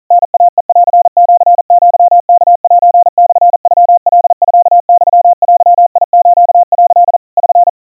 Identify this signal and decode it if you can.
Morse code (Japanese Wabun code), heard as a keyed tone, about 40 wpm, 700 Hz. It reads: タイヘンシアワセモノカミユデス」 <SN>
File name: kawa / Morse sample / kamyu 150 cpm 700 Hz Crispy